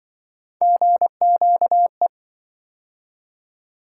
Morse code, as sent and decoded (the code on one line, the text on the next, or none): --. --.- .
GQE